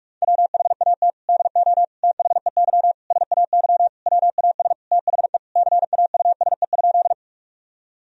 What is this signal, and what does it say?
Morse code, heard as a keyed tone, about 42 wpm, 700 Hz. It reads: WHAT DO THEY SAY WAS THE CAUSE?
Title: Sentences from Top 1000 Words - Review Part 1 - 45wpm